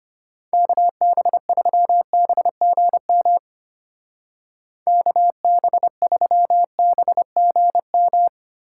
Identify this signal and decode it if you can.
Morse code, heard as a keyed tone, about 31 wpm, 700 Hz. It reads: KB3BGM KB3BGM
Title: Speed Racing - USA 2x3 Call Signs 25wpm